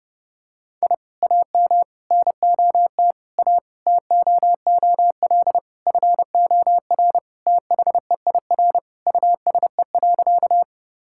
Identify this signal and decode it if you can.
Morse code, heard as a keyed tone, about 30 wpm, 700 Hz. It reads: I AM NOT A TOOL FOR THEIR USE.
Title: Sentences from Top 1000 Words - Review Part 6 - 30wpm